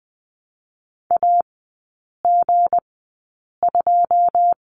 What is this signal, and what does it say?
Morse code, heard as a keyed tone, about 20 wpm, spaced as for 8 wpm, 700 Hz. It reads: AG2
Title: 166 Lesson 27e - Number 7 - Three Characters 20wpm